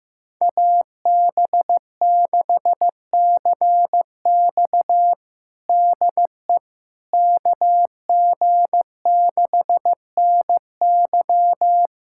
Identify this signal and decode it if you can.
Morse code, heard as a keyed tone, about 15 wpm, 700 Hz. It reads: AB6CX DE KG6NY
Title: USA Callsign DE Callsign - 5 Characters - 15wpm